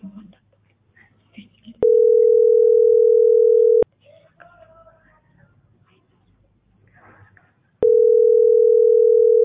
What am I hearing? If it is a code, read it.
Ringback tone (North America), 440 + 480 Hz.